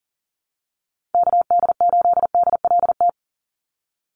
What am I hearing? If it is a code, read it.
Morse code, heard as a keyed tone, about 40 wpm, 700 Hz. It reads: KD8DLT
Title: USA Call Signs 40wpm